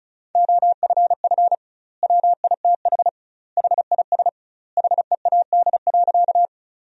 Morse code, heard as a keyed tone, about 34 wpm, 700 Hz. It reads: OFF WITH HIS HEAD.
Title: 2x Word-Spacing - Sentences from Top 300 Words - 35wpm